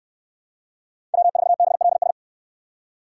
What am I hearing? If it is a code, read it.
Morse code, heard as a keyed tone, about 69 wpm, 700 Hz. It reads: W4BLS